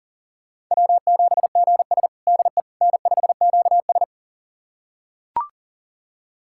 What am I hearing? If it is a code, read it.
Morse code, heard as a keyed tone, about 40 wpm, 700 Hz. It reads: W7GS DE N5QS
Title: USA Callsign DE Callsign - 4 Characters - 40wpm